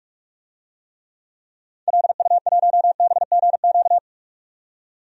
Morse code, heard as a keyed tone, about 46 wpm, 700 Hz. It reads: LU1BGQ
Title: DXCC-100 Call Signs - 6 Characters - 45wpm